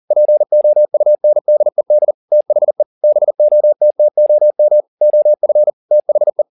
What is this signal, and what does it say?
Morse code, heard as a keyed tone, about 40 wpm, 600 Hz. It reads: POUNDED THE BOTTOM OF THE
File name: the old man and the sea 2 40wpm